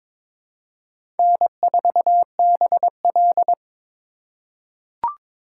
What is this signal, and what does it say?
Morse code, heard as a keyed tone, about 21 wpm, 700 Hz. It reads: N4BL
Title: USA Call Signs - 4 Characters - 2x Character Spacing - 22wpm